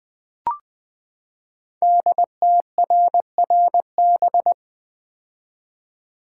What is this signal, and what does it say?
Morse code, heard as a keyed tone, about 20 wpm, 700 Hz. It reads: DTRRB
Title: Code Groups - 5 Letters - 20wpm